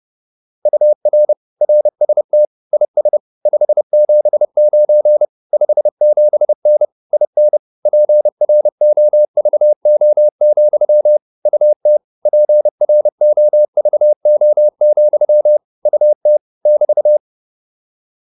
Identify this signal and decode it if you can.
Morse code, heard as a keyed tone, about 30 wpm, 600 Hz. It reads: UR RST IS 579 57N IN PROVO, UT PROVO, UT <BT>
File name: QSO301 30